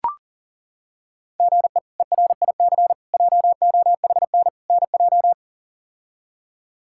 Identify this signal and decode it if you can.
Morse code, heard as a keyed tone, about 40 wpm, 700 Hz. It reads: GE ERIC JOHN NJ
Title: K1USN SST QSO Practice - 40wpm